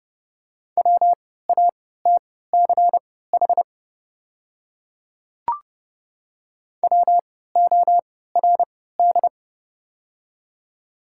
Morse code, heard as a keyed tone, about 30 wpm, spaced as for 17 wpm, 700 Hz. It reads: WATCH WORD
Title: 3x Character Spacing - Top 500 Words - 30wpm